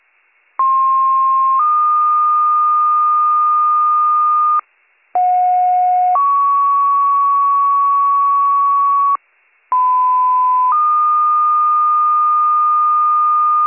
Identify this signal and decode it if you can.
Two-tone page, heard as a one-second tone, then a three-second tone, about 1060 then 1220 Hz, about 730 then 1120 Hz, about 1010 then 1220 Hz. It is heard from 0.6 to 13.7 s.